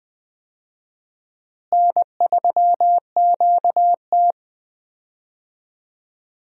Morse code, heard as a keyed tone, about 19 wpm, 700 Hz. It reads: N3QT
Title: USA Call Signs - 4 Characters - Repeated 3x - 20wpm